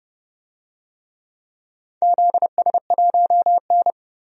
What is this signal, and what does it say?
Morse code, heard as a keyed tone, about 31 wpm, 700 Hz. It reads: ZS1N
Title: DXCC-100 Call Signs - 4 Characters - 30wpm